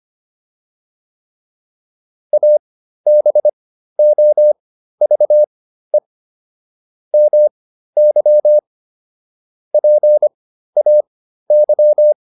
Morse code, heard as a keyed tone, about 25 wpm, spaced as for 13 wpm, 600 Hz. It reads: ABOVE MY PAY